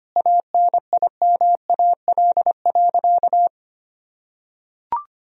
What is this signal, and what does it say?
Morse code, heard as a keyed tone, about 25 wpm, 700 Hz. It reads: ANIMAL.
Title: Sentences from Top 1000 Words - Review Part 1 - 25wpm